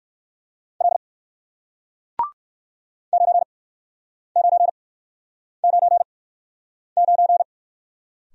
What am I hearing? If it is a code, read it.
Morse code, heard as a keyed tone, about 57 wpm, spaced as for 9 wpm, 700 Hz. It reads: L 9999